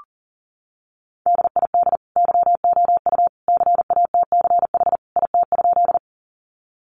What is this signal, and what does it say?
Morse code, heard as a keyed tone, about 40 wpm, 700 Hz. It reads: DID YOU CATCH IT?
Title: Sentences from Top 1500 Words 40wpm